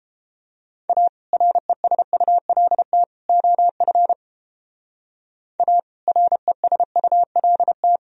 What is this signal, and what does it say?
Morse code, heard as a keyed tone, about 29 wpm, 700 Hz. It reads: A RESULT OF A RESULT